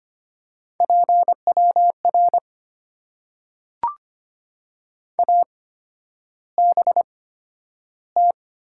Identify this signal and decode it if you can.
Morse code, heard as a keyed tone, about 25 wpm, spaced as for 11 wpm, 700 Hz. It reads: PWR ABT
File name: Top 90 QSO Elements - 8x Character Spacing - 25wpm